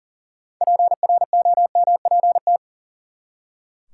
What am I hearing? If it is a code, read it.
Morse code, heard as a keyed tone, about 40 wpm, 700 Hz. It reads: PROMPT
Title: Six Letter Words - 40wpm